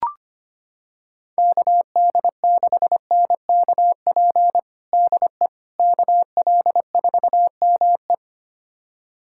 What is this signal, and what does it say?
Morse code, heard as a keyed tone, about 25 wpm, 700 Hz. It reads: KD6NKP DE KL4ME
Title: USA Callsign DE Callsign 25wpm